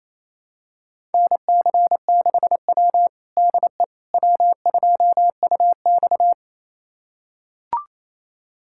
Morse code, heard as a keyed tone, about 28 wpm, 700 Hz. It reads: NC6W DE W2UX
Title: USA Callsign DE Callsign - 4 Characters - 28wpm